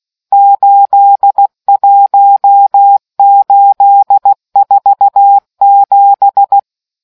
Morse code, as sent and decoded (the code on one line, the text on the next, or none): ---.. .---- ---.. ....- --...
81847